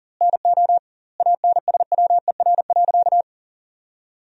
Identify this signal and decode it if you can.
Morse code, heard as a keyed tone, about 40 wpm, 700 Hz. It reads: NO ANSWER.
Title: 2x Word-Spacing - Sentences from Top 500 Words - 40wpm